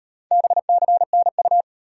Morse code, heard as a keyed tone, about 36 wpm, 700 Hz. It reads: BCNU